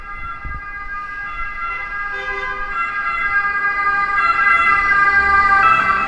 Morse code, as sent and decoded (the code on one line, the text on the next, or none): none